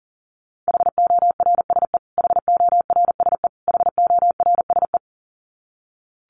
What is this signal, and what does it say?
Morse code, heard as a keyed tone, about 40 wpm, 700 Hz. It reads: HORSE HORSE HORSE
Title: Top 500 Words Repeated 40wpm